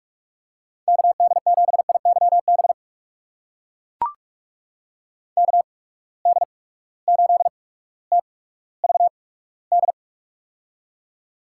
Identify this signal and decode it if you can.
Morse code, heard as a keyed tone, about 44 wpm, spaced as for 19 wpm, 700 Hz. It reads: KD7IYB KD8TVD
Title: USA Call Signs - 6 Characters - 8x Character Spacing - 45wpm